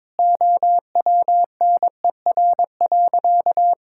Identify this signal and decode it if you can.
Morse code, heard as a keyed tone, about 22 wpm, 700 Hz. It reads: OWNER.
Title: Sentences from Top 1600 Words 22wpm